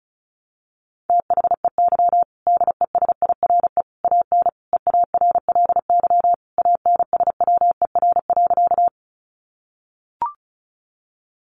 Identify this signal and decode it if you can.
Morse code, heard as a keyed tone, about 34 wpm, 700 Hz. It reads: THEY DESIRE AN EARLY ANSWER.
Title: Sentences from Top 1100 Words 35wpm